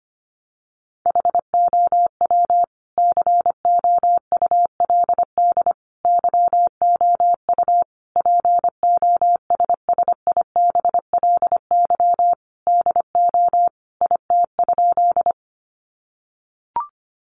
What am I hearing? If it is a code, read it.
Morse code, heard as a keyed tone, about 25 wpm, 700 Hz. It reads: HOW COULD YOU POSSIBLY DO IT?
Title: Sentences from Top 1300 Words 25wpm